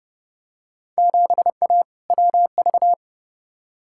Morse code, heard as a keyed tone, about 29 wpm, 700 Hz. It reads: 7A WV
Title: Field Day Exchanges Encore - 30wpm